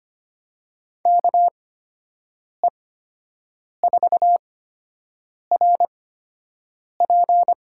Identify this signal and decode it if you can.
Morse code, heard as a keyed tone, about 25 wpm, spaced as for 7 wpm, 700 Hz. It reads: KE4RP
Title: USA Call Signs - 6 Characters - 8x Character Spacing - 25wpm